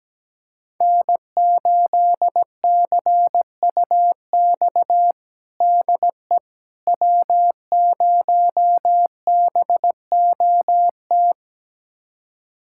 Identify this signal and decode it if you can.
Morse code, heard as a keyed tone, about 17 wpm, 700 Hz. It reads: N8CUX DE W0BOT
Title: USA Callsign DE Callsign - 5 Characters - 17wpm